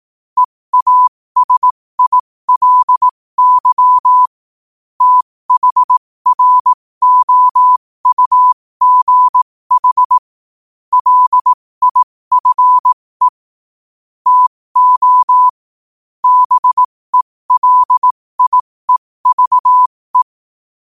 Morse code, heard as a keyed tone, about 18 wpm, spaced as for 15 wpm, 1000 Hz. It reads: EASILY THROUGH LIFE TO BELIEVE
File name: qotdfiles / 15wpm